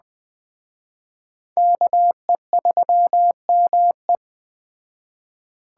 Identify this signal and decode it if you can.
Morse code, heard as a keyed tone, about 20 wpm, 700 Hz. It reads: KE3ME